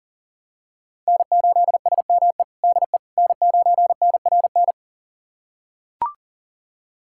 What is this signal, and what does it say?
Morse code, heard as a keyed tone, about 37 wpm, 700 Hz. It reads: N8SME DE N9NRN
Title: USA Callsign DE Callsign - 5 Characters - 40wpm